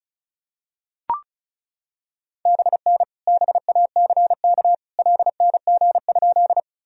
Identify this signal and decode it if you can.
Morse code, heard as a keyed tone, about 36 wpm, 700 Hz. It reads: BN BACK LNG?
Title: Sentences from Top 200 Words and Abbreviations 35wpm